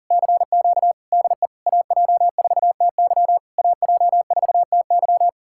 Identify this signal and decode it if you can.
Morse code, heard as a keyed tone, about 51 wpm, 700 Hz. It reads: CQ DE AJ4TY AJ4TY